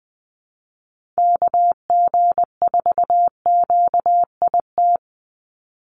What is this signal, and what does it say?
Morse code, heard as a keyed tone, about 20 wpm, 700 Hz. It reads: KG4QIT
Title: USA Call Signs 20wpm